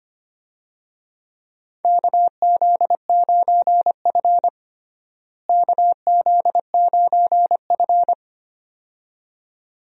Morse code, heard as a keyed tone, about 25 wpm, 700 Hz. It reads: KZ9F KZ9F
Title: USA Call Signs - 4 Characters - Repeated 2x - 25wpm